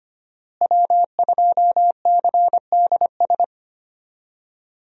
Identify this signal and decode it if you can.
Morse code, heard as a keyed tone, about 25 wpm, 700 Hz. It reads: W2CDS
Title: USA Call Signs - 5 Characters - Repeated 3x - 25wpm